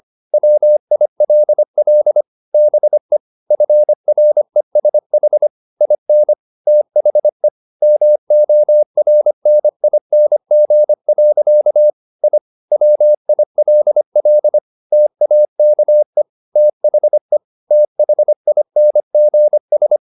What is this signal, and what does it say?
Morse code, heard as a keyed tone, about 25 wpm, 600 Hz. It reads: WILL BE FRESH IN THE MORNING. I WILL TAKE THE THINGS
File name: the old man and the sea part 1 25wpm